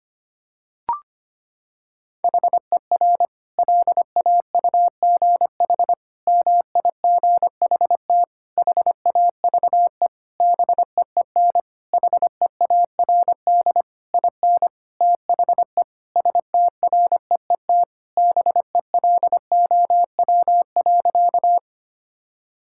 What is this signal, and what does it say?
Morse code, heard as a keyed tone, about 25 wpm, 700 Hz. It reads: HER LAUGH MIGHT HAVE BEEN HEARD IN THE STREET BELOW.
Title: Sentences from Top 1000 Words - Review Part 5 - 25wpm